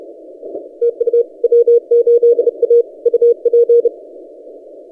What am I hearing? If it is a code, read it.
Morse code, heard as a keyed tone, about 31 wpm, 470 Hz. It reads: TU W8A UP
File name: W8A (American Samoa) 40CW